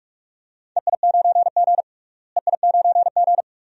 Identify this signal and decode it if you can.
Morse code, heard as a keyed tone, about 45 wpm, 700 Hz. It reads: EI9G EI9G
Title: DXCC-100 Call Signs - 4 Characters - Repeated 2x - 45wpm